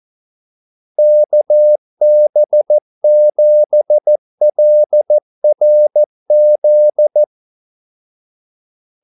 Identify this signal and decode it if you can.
Morse code, heard as a keyed tone, about 14 wpm, 600 Hz. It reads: KB7LRZ
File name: Callsigns-14wpm